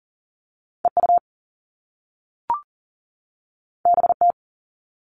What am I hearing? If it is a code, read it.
Morse code, heard as a keyed tone, about 41 wpm, 700 Hz. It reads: EU BT
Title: All Cut Numbers - 2-Digit 40wpm